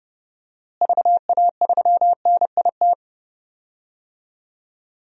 Morse code, heard as a keyed tone, about 30 wpm, 700 Hz. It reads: VA3NIT
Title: DXCC-100 Call Signs - 6 Characters - Repeated 2x - 30wpm